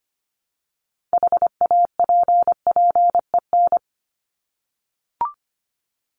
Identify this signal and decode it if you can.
Morse code, heard as a keyed tone, about 25 wpm, 700 Hz. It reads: HAPPEN
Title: ED Suffix - Top 100 Words with Base Words 25wpm